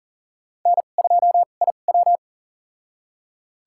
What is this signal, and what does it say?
Morse code, heard as a keyed tone, about 40 wpm, spaced as for 29 wpm, 700 Hz. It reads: N2IW
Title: USA Call Signs - 4 Characters - 2x Character Spacing - 40wpm